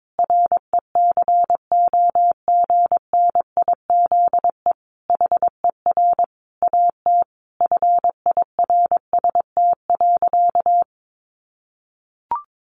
Morse code, heard as a keyed tone, about 22 wpm, 700 Hz. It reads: RECOGNIZE HER AT FIRST.
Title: Sentences from Top 1400 Words 22wpm